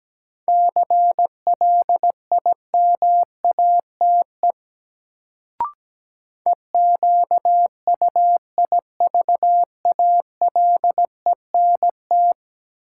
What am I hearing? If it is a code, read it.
Morse code, heard as a keyed tone, about 17 wpm, 700 Hz. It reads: CLIMATE EQUIVALENT
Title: Sentences from Top 1600 Words - New Word Drill - 17wpm